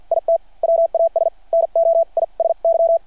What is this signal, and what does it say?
Morse code, heard as a keyed tone, about 52 wpm, spaced as for 42 wpm, 660 Hz. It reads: IT WAS NOISY
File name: archived / CRSnet